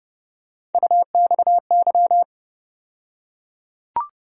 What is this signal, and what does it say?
Morse code, heard as a keyed tone, about 30 wpm, 700 Hz. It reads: UXY
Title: USA Call Sign Postfixes - 3 Letters 30wpm